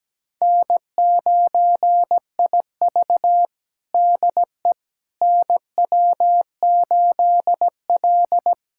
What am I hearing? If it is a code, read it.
Morse code, heard as a keyed tone, about 17 wpm, 700 Hz. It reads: N9IV DE NW8L
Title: USA Callsign DE Callsign - 4 Characters - 17wpm